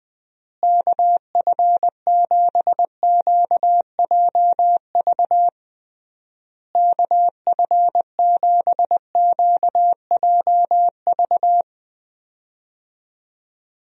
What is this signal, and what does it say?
Morse code, heard as a keyed tone, about 20 wpm, 700 Hz. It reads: KF7QJV KF7QJV
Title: USA Call Signs - 6 Characters - Repeated 3x - 20wpm